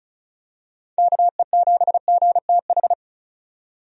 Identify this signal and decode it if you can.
Morse code, heard as a keyed tone, about 36 wpm, 700 Hz. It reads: KE7GTH
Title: USA Call Signs - 6 Characters - 35wpm